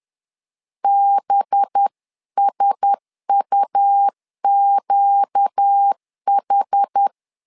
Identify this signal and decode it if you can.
Morse code, heard as a keyed tone, about 10 wpm, 800 Hz. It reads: BSUQH
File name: LEZIONI / 5 LBVC / 5 LBV 10wpm 2248